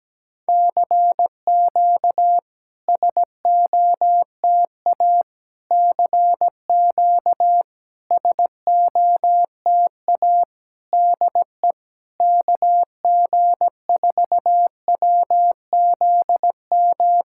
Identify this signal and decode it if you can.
Morse code, heard as a keyed tone, about 17 wpm, 690 Hz. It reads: CQ SOTA CQ SOTA DE KG4WZM
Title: SOTA QSO Practice 17wpm